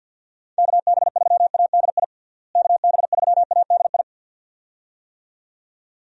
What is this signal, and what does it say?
Morse code, heard as a keyed tone, about 51 wpm, 700 Hz. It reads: KB3ADI KB3ADI
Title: USA Call Signs - 6 Characters - Repeated 2x - 50wpm